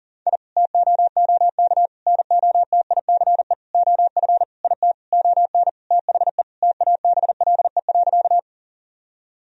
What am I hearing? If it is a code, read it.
Morse code, heard as a keyed tone, about 40 wpm, 700 Hz. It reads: I TOOK NOTICE OF IT ON THE TABLE.
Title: Sentences from Top 1000 Words - Review Part 2 - 40wpm